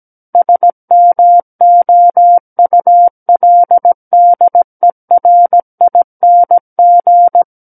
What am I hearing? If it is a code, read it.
Morse code, heard as a keyed tone, about 17 wpm, 700 Hz. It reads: SMOULDERING
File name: Book One / 08-Chapter 8 - 17 WPM